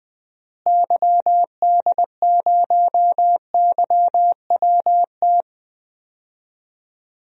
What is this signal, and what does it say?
Morse code, heard as a keyed tone, about 20 wpm, 700 Hz. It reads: YD0YWT
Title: DXCC-100 Call Signs - 6 Characters - Repeated 3x - 20wpm